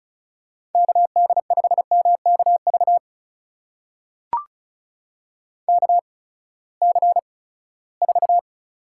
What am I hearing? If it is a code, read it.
Morse code, heard as a keyed tone, about 36 wpm, spaced as for 20 wpm, 700 Hz. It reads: KD5MKV KC4
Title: USA Call Signs - 6 Characters - 8x Character Spacing - 35wpm